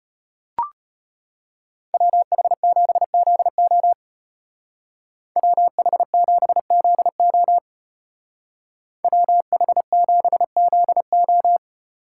Morse code, heard as a keyed tone, about 33 wpm, 700 Hz. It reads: WH7ZO WH7ZO WH7ZO